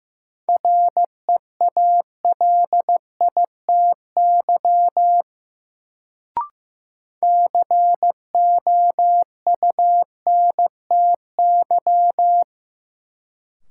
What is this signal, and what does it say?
Morse code, heard as a keyed tone, about 15 wpm, 700 Hz. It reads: REALITY COUNTY
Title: Sentences from Top 1200 Words - New Word Drill - 15wpm